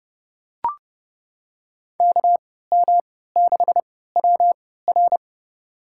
Morse code, heard as a keyed tone, about 31 wpm, spaced as for 17 wpm, 700 Hz. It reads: KM6WR
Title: USA Call Signs - 3x Character Spacing - 30wpm